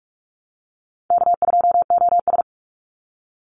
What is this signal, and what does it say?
Morse code, heard as a keyed tone, about 44 wpm, 700 Hz. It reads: K2OS
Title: Easy Call Signs - 45wpm